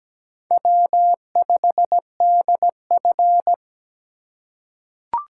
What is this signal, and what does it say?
Morse code, heard as a keyed tone, about 17 wpm, 700 Hz. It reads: W5DF
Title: USA Call Signs - 4 Characters - 2x Character Spacing - 17wpm